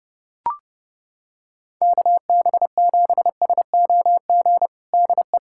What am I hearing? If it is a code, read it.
Morse code, heard as a keyed tone, about 30 wpm, 700 Hz. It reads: KB7SOG DE
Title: USA Callsign DE Callsign 30wpm